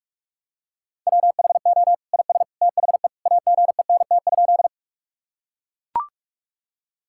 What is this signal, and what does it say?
Morse code, heard as a keyed tone, about 42 wpm, 700 Hz. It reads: WHO IS THE AGENT?
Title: Sentences from Top 1600 Words 45wpm